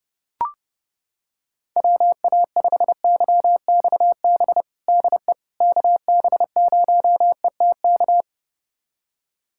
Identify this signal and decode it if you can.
Morse code, heard as a keyed tone, about 32 wpm, 700 Hz. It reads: WA5YXB DE KB0ETK
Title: USA Callsign DE Callsign - 6 Characters - 30wpm